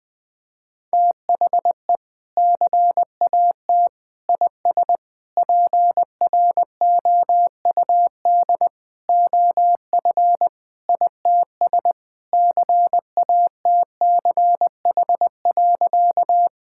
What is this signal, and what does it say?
Morse code, heard as a keyed tone, about 20 wpm, 700 Hz. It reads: THE CAT IS PROUD OF ITS CATCH.
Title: Sentences with 2-Word Phrases - Spoken Beginning - 20wpm